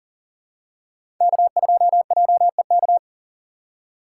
Morse code, heard as a keyed tone, about 40 wpm, 690 Hz. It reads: K2JEK
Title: USA Call Signs - 5 Characters - 6x Character Spacing - 40wpm